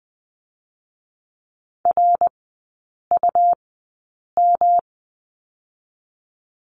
Morse code, heard as a keyed tone, about 20 wpm, 700 Hz. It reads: R U M
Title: 107 Lesson 18e - Letter m - Three Characters 20wpm